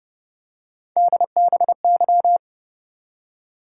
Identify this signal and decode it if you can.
Morse code, heard as a keyed tone, about 33 wpm, 700 Hz. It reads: DBY